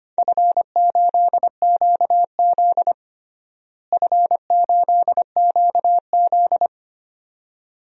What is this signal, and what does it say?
Morse code, heard as a keyed tone, about 25 wpm, 700 Hz. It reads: F8QZ F8QZ
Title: DXCC-100 Call Signs - 4 Characters - Repeated 2x - 25wpm